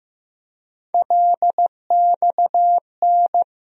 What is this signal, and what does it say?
Morse code, heard as a keyed tone, about 15 wpm, 700 Hz. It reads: LXN
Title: USA Call Sign Postfixes - 3 Letters 15wpm